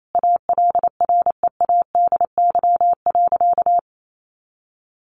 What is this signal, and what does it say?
Morse code, heard as a keyed tone, about 28 wpm, 700 Hz. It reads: ALREADY.
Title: Sentences from Top 1400 Words 28wpm